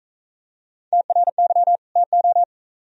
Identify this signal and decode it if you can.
Morse code, heard as a keyed tone, about 43 wpm, 700 Hz. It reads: TRY TO